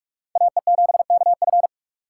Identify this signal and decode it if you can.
Morse code, heard as a keyed tone, about 57 wpm, 700 Hz. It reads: AE7KF